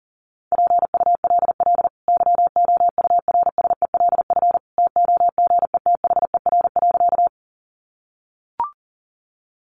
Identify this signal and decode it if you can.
Morse code, heard as a keyed tone, about 40 wpm, 700 Hz. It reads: PULL YOURSELF TOGETHER.
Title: Sentences from Top 600 Words 40wpm